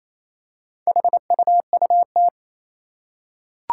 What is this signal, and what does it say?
Morse code, heard as a keyed tone, about 28 wpm, 700 Hz. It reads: HUUT
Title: Four Challenging Letter-Number Combinations 28wpm